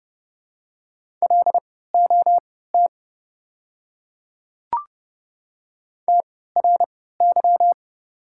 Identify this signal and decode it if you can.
Morse code, heard as a keyed tone, about 30 wpm, spaced as for 17 wpm, 690 Hz. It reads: LOT TRY
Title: 3x Character Spacing - Top 500 Words - 30wpm